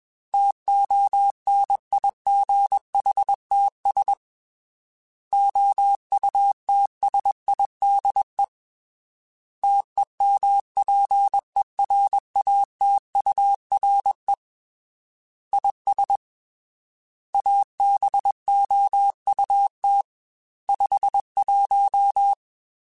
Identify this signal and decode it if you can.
Morse code, heard as a keyed tone, about 21 wpm, 800 Hz. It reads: TONIGHTS OUTSIDE TEMPERATURE IS ABOUT 51
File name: Weather Sets / Weather Sets-mp3s / Weather 4 21 @20